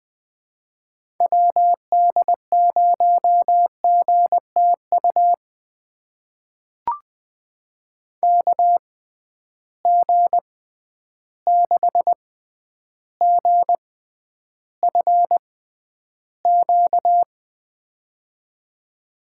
Morse code, heard as a keyed tone, about 20 wpm, spaced as for 10 wpm, 700 Hz. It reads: WD0GTU KG6GFQ
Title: USA Call Signs - 6 Characters - 6x Character Spacing - 20wpm